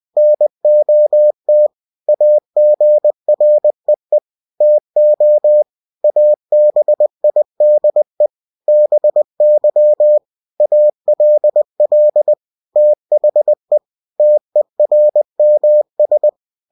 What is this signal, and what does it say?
Morse code, heard as a keyed tone, about 20 wpm, 600 Hz. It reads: NOT AGREE TO ABIDE BY ALL THE TERMS
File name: cw-books / winnie-the-pooh / winnie-20wpm-page-0113